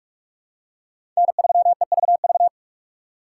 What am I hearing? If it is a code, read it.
Morse code, heard as a keyed tone, about 44 wpm, 700 Hz. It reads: N3EVV
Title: USA Call Signs - 5 Characters - 4x Character Spacing - 45wpm